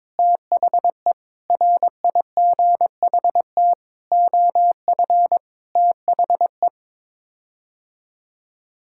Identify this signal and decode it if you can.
Morse code, heard as a keyed tone, about 22 wpm, 700 Hz. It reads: THE RIGHT OF THE
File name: Sets of 9 Words - 22wpm